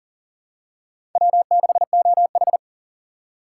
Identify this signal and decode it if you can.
Morse code, heard as a keyed tone, about 41 wpm, 700 Hz. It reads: W6OH
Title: USA Call Signs - 4 Characters - 2x Character Spacing - 40wpm